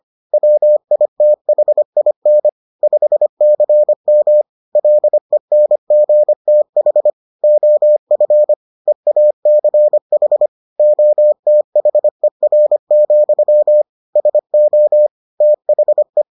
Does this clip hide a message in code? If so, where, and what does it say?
Yes: Morse code from 0.3 to 16.2 s, heard as a keyed tone, about 25 wpm, 600 Hz. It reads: WITHIN 5CM LENGTH OF EACH OTHER, SO THE